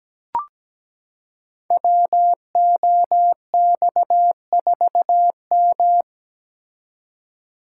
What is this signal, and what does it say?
Morse code, heard as a keyed tone, about 17 wpm, 700 Hz. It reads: WOX4M
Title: Code Groups - 5 Letter:Numbers - 17wpm